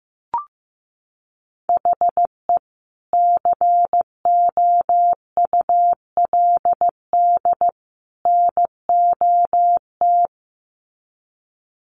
Morse code, heard as a keyed tone, about 15 wpm, 700 Hz. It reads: HE COULD NOT
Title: Sets of 3 Words 15wpm